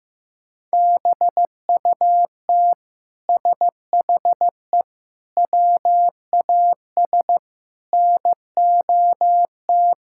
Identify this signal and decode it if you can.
Morse code, heard as a keyed tone, about 15 wpm, 700 Hz. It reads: BUT SHE WAS NOT
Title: Sentences from Top 1000 Words - Review Part 6 - 15wpm